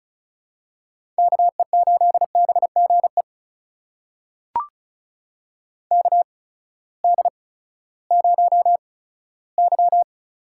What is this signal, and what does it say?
Morse code, heard as a keyed tone, about 36 wpm, spaced as for 18 wpm, 700 Hz. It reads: KE8BGE KD0Y
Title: USA Call Signs - 6 Characters - 8x Character Spacing - 35wpm